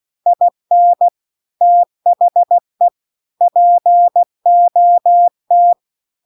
Morse code, heard as a keyed tone, about 16 wpm, 700 Hz. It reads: IN THE POT
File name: CW-The-Big-Twoheaded-River-2-16WPM